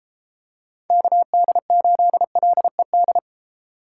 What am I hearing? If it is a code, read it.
Morse code, heard as a keyed tone, about 31 wpm, 700 Hz. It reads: KD8LED